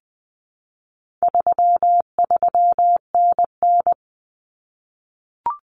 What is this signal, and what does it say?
Morse code, heard as a keyed tone, about 20 wpm, 700 Hz. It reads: 33NN
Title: Common Cut Numbers - 4-Digit 20wpm